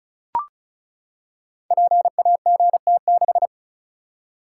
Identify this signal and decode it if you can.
Morse code, heard as a keyed tone, about 36 wpm, 700 Hz. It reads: PAGT6
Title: Code Groups - 5 Letter:Numbers - 35wpm